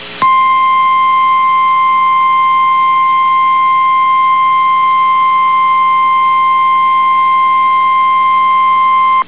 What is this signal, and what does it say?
Emergency Alert System (SAME) tones (the 1050 Hz attention tone). It is heard from 0.5 to 8.7 s.
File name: eas received alerts / L1 19525 audio